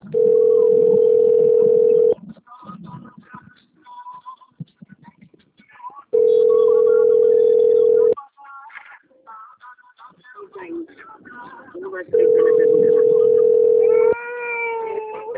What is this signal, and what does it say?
Ringback tone (North America), 440 + 480 Hz.